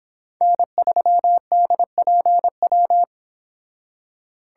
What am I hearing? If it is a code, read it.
Morse code, heard as a keyed tone, about 25 wpm, 700 Hz. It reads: N3DPW